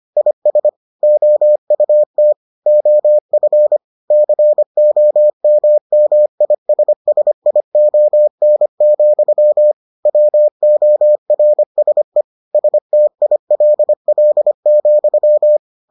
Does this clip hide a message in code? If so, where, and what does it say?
Yes: Morse code from 0.2 to 15.6 s, heard as a keyed tone, about 25 wpm, 600 Hz. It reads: IS OUT OF COMMISSION, WORSE STILL,